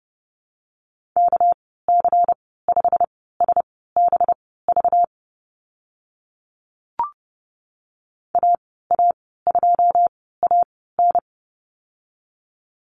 Morse code, heard as a keyed tone, about 29 wpm, spaced as for 17 wpm, 700 Hz. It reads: KC5SBV AA2AN
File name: Lesson 37g - Letter x - Call Signs 30wpm